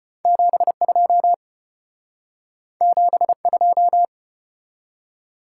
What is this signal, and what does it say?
Morse code, heard as a keyed tone, about 29 wpm, 700 Hz. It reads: 72 72
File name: practice / 25wpm / Speed-Racing - Two Letter-Number Combinations - 25wpm